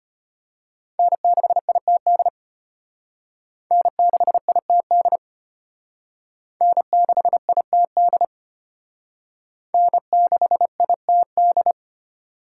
Morse code, heard as a keyed tone, about 32 wpm, 700 Hz. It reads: N6ITD N6ITD N6ITD N6ITD